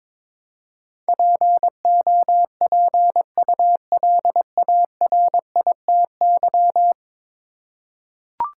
Morse code, heard as a keyed tone, about 22 wpm, 700 Hz. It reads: POPULARITY
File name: practice / 22wpm / 10 Letter Words - First 3 Letters Given - 22wpm